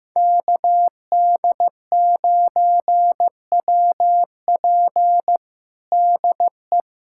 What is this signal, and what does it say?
Morse code, heard as a keyed tone, about 15 wpm, 700 Hz. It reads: KD9WP DE
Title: USA Callsign DE Callsign - 5 Characters - 15wpm